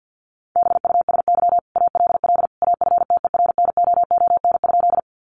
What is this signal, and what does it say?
Morse code, heard as a keyed tone, about 50 wpm, 700 Hz. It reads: BUSY ALL AFTERNOON?
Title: Sentences from Top 1500 Words 50wpm